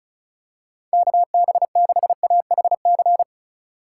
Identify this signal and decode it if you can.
Morse code, heard as a keyed tone, about 36 wpm, 700 Hz. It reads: KB6AHC
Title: USA Call Signs - 6 Characters - 2x Character Spacing - 35wpm